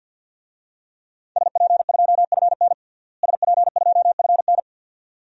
Morse code, heard as a keyed tone, about 52 wpm, 700 Hz. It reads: SP2FN SP2FN
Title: DXCC-100 Call Signs - 5 Characters - Repeated 2x - 50wpm